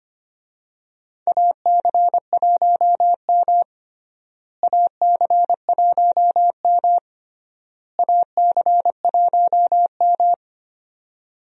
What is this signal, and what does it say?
Morse code, heard as a keyed tone, about 25 wpm, 700 Hz. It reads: AC1M AC1M AC1M
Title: USA Call Signs - 4 Characters - Repeated 3x - 25wpm